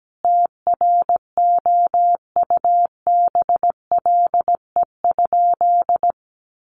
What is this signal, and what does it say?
Morse code, heard as a keyed tone, about 17 wpm, 700 Hz. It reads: TROUBLE?
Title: Sentences from Top 1200 Words 17wpm